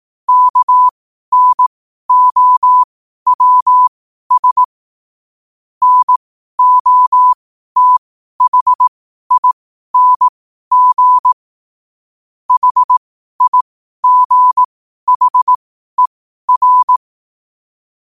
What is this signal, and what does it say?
Morse code, heard as a keyed tone, about 18 wpm, spaced as for 12 wpm, 1000 Hz. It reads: KNOWS NOTHING HIGHER
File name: qotdfiles / 12wpm